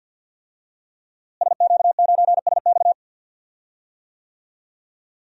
Morse code, heard as a keyed tone, about 47 wpm, 700 Hz. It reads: SQ9SX
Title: DXCC-100 Call Signs - 5 Characters - Repeated 3x - 50wpm